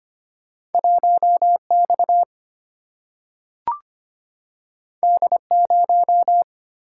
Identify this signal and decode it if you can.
Morse code, heard as a keyed tone, about 25 wpm, 700 Hz. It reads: 1X D0
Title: Single-Letter Single-Number 25wpm